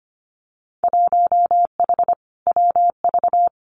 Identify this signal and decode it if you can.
Morse code, heard as a keyed tone, about 25 wpm, 700 Hz. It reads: 1H WV
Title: Winter Field Day Exchanges 25wpm